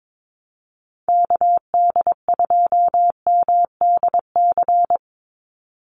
Morse code, heard as a keyed tone, about 22 wpm, 700 Hz. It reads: KD2MDC
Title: USA Call Signs 22wpm